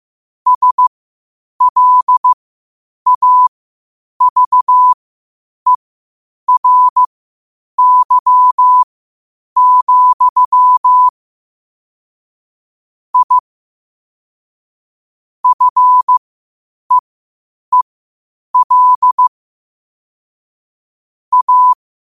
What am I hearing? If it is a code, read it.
Morse code, heard as a keyed tone, about 15 wpm, spaced as for 8 wpm, 1000 Hz. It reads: SLAVERY, I FEEL A
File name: qotdfiles / 8wpm